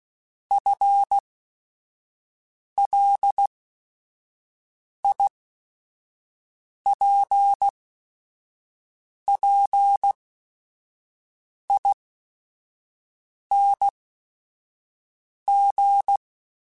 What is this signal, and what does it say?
Morse code, heard as a keyed tone, about 16 wpm, spaced as for 5 wpm, 800 Hz. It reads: FLIPPING
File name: FL / FL mp3s / Fast FL @5